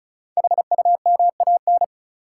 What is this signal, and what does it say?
Morse code, heard as a keyed tone, about 36 wpm, 700 Hz. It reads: HUMAN
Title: Sets of 5 Words - New Word Drill - 35wpm